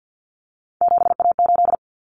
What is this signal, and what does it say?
Morse code, heard as a keyed tone, about 50 wpm, 700 Hz. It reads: 7A8
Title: Common Cut Numbers - 3-Digit 50wpm